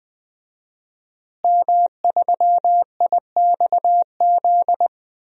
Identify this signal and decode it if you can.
Morse code, heard as a keyed tone, about 20 wpm, 700 Hz. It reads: M3IXZ
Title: DXCC-100 Call Signs - 5 Characters - Repeated 3x - 20wpm